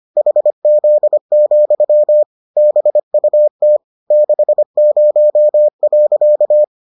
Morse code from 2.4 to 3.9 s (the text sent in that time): BUT